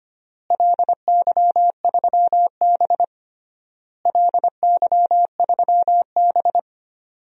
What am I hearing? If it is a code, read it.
Morse code, heard as a keyed tone, about 25 wpm, 700 Hz. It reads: LY3B LY3B
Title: DXCC-100 Call Signs - 4 Characters - Repeated 3x - 25wpm